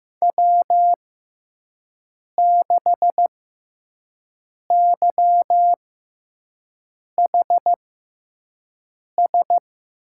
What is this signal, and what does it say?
Morse code, heard as a keyed tone, about 15 wpm, spaced as for 5 wpm, 700 Hz. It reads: W6YHS
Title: USA Call Signs - 6x Character Spacing - 15wpm